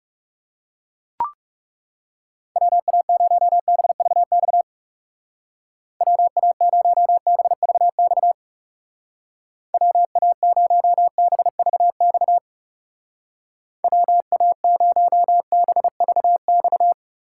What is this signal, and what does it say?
Morse code, heard as a keyed tone, about 36 wpm, 700 Hz. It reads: WA0BVX WA0BVX WA0BVX WA0BVX
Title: Speed Racing - USA Call Signs 30wpm